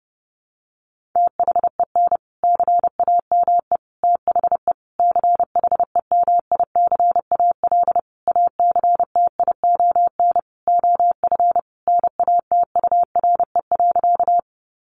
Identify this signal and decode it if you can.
Morse code, heard as a keyed tone, about 30 wpm, 700 Hz. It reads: THEN CAME THE CHEMICAL ACTION OF NATURE.
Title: Sentences from Top 1200 Words 30wpm